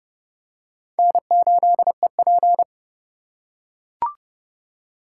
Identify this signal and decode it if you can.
Morse code, heard as a keyed tone, about 30 wpm, 700 Hz. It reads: N8EP
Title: USA Call Signs - 8x Character Spacing - 30wpm